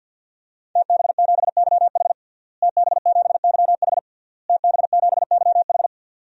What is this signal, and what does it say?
Morse code, heard as a keyed tone, about 53 wpm, 700 Hz. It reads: TB7YH TB7YH TB7YH
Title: DXCC-100 Call Signs - 5 Characters - Repeated 3x - 50wpm